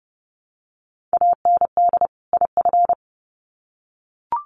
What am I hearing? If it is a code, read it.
Morse code, heard as a keyed tone, about 30 wpm, 700 Hz. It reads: AND IF
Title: Sets of 2 Words 30wpm